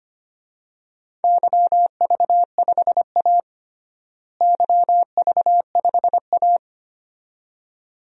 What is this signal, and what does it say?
Morse code, heard as a keyed tone, about 25 wpm, 700 Hz. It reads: YV5A YV5A
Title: DXCC-100 Call Signs - 4 Characters - Repeated 2x - 25wpm